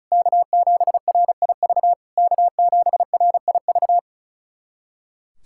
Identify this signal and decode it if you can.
Morse code, heard as a keyed tone, about 36 wpm, 700 Hz. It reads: K7RIV K7RIV
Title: SOTA QSO Practice 35wpm